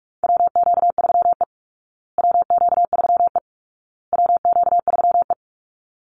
Morse code, heard as a keyed tone, about 44 wpm, 700 Hz. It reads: WQ3E WQ3E WQ3E
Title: Call Signs Repeated 2x - 45wpm